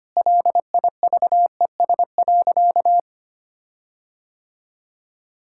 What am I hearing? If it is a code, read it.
Morse code, heard as a keyed tone, about 25 wpm, 700 Hz. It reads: LIVES.
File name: practice / 25wpm / Difficult Sentences with Hints - 25wpm